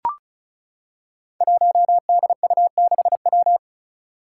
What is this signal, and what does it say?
Morse code, heard as a keyed tone, about 34 wpm, 700 Hz. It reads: 1DU6W
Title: Five Challenging Letter-Number Combinations 35wpm